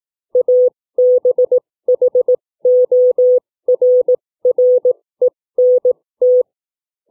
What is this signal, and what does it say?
Morse code, heard as a keyed tone, about 18 wpm, spaced as for 15 wpm, 500 Hz. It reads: ABHORRENT
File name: CW / book reading